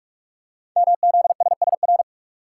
Morse code, heard as a keyed tone, about 44 wpm, 700 Hz. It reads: MZSSR
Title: Code Groups - 5 Letters - 45wpm